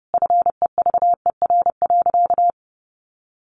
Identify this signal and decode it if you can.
Morse code, heard as a keyed tone, about 30 wpm, 700 Hz. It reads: FEVER.
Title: Easy Phrases 30wpm